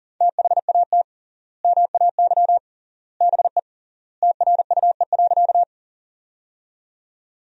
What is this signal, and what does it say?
Morse code, heard as a keyed tone, about 40 wpm, 700 Hz. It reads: THAT MAY BE TRUE.
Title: 3x Word-Spacing - Sentences from Top 400 Words - 40wpm